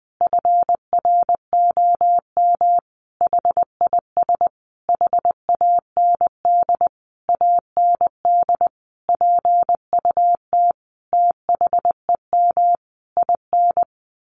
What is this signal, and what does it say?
Morse code, heard as a keyed tone, about 20 wpm, 700 Hz. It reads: FROM HIS HAND AND PUT THEM IN
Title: Sentences from Top 1300 Words 20wpm